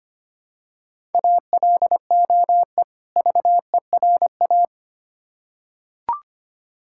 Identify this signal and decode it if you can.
Morse code, heard as a keyed tone, about 25 wpm, 700 Hz. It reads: ALOE VERA
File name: practice / 25wpm / Difficult Words with Hints - 25wpm